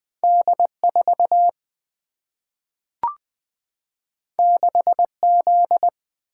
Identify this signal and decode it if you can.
Morse code, heard as a keyed tone, about 20 wpm, 700 Hz. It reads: D 4 6 Z
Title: Single-Letter Single-Number 20wpm